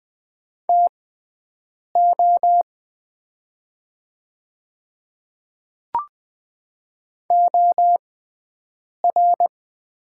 Morse code, heard as a keyed tone, about 20 wpm, spaced as for 7 wpm, 700 Hz. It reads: TO OR
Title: 6x Character Spacing - Top 200 Words - 20wpm